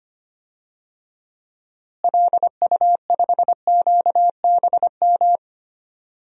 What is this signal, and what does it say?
Morse code, heard as a keyed tone, about 25 wpm, 700 Hz. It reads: LU5QBM
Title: DXCC-100 Call Signs - 6 Characters - Repeated 2x - 25wpm